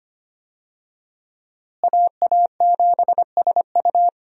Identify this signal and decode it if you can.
Morse code, heard as a keyed tone, about 25 wpm, 700 Hz. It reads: AA7SU
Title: USA Call Signs - 5 Characters - Repeated 3x - 25wpm